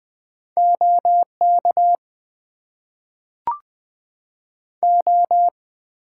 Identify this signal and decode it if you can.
Morse code, heard as a keyed tone, about 20 wpm, 700 Hz. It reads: OK O
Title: Top 90 QSO Elements - 6x Character Spacing - 20wpm